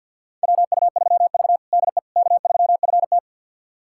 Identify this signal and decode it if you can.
Morse code, heard as a keyed tone, about 48 wpm, 700 Hz. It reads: WU3V DE K3FT
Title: USA Callsign DE Callsign - 4 Characters - 50wpm